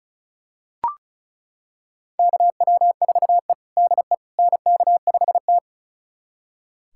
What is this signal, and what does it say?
Morse code, heard as a keyed tone, about 36 wpm, 700 Hz. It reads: KW4E DE NK5T
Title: USA Callsign DE Callsign - 4 Characters - 35wpm